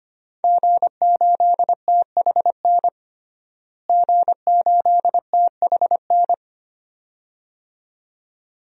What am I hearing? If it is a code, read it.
Morse code, heard as a keyed tone, about 25 wpm, 700 Hz. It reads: G8THN G8THN
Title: DXCC-100 Call Signs - 5 Characters - Repeated 2x - 25wpm